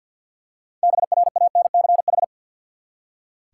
Morse code, heard as a keyed tone, about 46 wpm, 700 Hz. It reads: BRANCH
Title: Six Letter Words - 50wpm